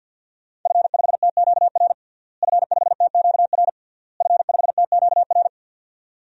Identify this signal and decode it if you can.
Morse code, heard as a keyed tone, about 53 wpm, 700 Hz. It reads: F5TQR F5TQR F5TQR
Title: DXCC-100 Call Signs - 5 Characters - Repeated 3x - 50wpm